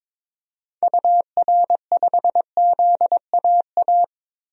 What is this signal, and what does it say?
Morse code, heard as a keyed tone, about 22 wpm, 700 Hz. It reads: UR5ZAA
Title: DXCC-100 Call Signs - 6 Characters - 22wpm